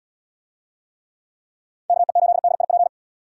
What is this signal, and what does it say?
Morse code, heard as a keyed tone, about 78 wpm, 700 Hz. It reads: KE8DIP